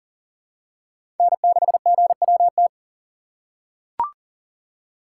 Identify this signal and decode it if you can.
Morse code, heard as a keyed tone, about 40 wpm, 700 Hz. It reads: N6GWT
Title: USA Call Signs - 5 Characters - 2x Character Spacing - 40wpm